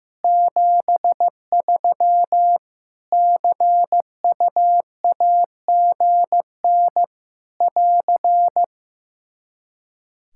Morse code, heard as a keyed tone, about 15 wpm, 700 Hz. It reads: 73 CUAGN <AR>
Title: Practice QSO - Exchange 4 - Ending - 15wpm